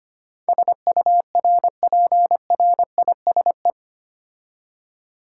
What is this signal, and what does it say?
Morse code, heard as a keyed tone, about 25 wpm, 700 Hz. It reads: SURPRISE
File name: practice / 25wpm / 8 Letter Words - First 3 Letters Given - 25wpm